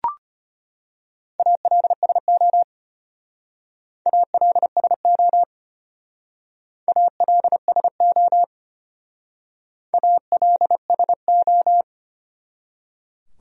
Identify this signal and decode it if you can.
Morse code, heard as a keyed tone, about 33 wpm, 700 Hz. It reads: ALSO ALSO ALSO ALSO